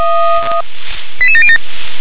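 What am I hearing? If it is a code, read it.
Five-tone selective call (ZVEI), heard as quick runs of five tones, tones of about 70 ms. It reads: 88707
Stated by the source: ZVEI five-tone selective call received on SDR, code 88705.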